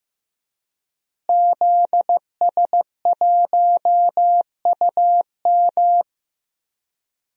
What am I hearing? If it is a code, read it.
Morse code, heard as a keyed tone, about 15 wpm, 700 Hz. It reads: ZS1UM
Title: DXCC-100 Call Signs - 5 Characters - Repeated 3x - 15wpm